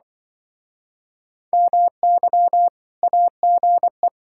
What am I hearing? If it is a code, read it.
Morse code, heard as a keyed tone, about 27 wpm, 700 Hz. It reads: MY AGE